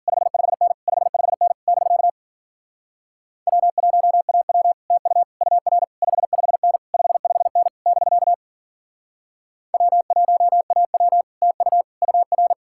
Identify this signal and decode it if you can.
Morse code, heard as a keyed tone, about 53 wpm, 700 Hz. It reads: 55N 55N <BK> W1AW TU UR 55N 55N <BK> W1AW TU UR